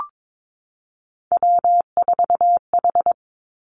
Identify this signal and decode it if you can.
Morse code, heard as a keyed tone, about 22 wpm, 700 Hz. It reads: W4H
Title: Three Challenging Letter-Number Combinations 22wpm